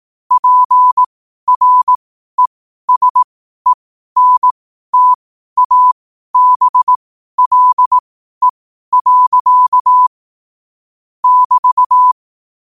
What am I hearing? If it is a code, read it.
Morse code, heard as a keyed tone, about 18 wpm, spaced as for 12 wpm, 1000 Hz. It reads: PRESENTABLE. <BT>
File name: qotdfiles / 12wpm